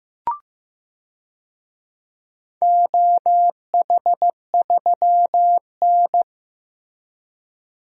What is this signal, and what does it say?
Morse code, heard as a keyed tone, about 15 wpm, 700 Hz. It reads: OH3N
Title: DXCC-100 Call Signs - 4 Characters - Repeated 3x - 15wpm